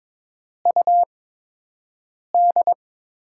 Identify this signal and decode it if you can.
Morse code, heard as a keyed tone, about 22 wpm, 700 Hz. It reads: U D
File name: Top 20 QSO Elements - 8x Character Spacing - 22wpm